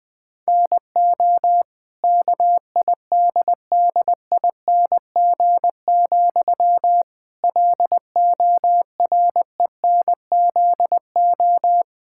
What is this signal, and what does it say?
Morse code, heard as a keyed tone, about 21 wpm, 700 Hz. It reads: NO KIDDING, LORENZO
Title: Pangram Challenge v2 - 20wpm